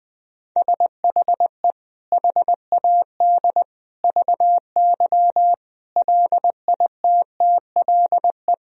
Morse code, heard as a keyed tone, about 19 wpm, 700 Hz. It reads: SHE HAD VY LITTLE
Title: Sentences from Top 400 Words and Abbreviations 20wpm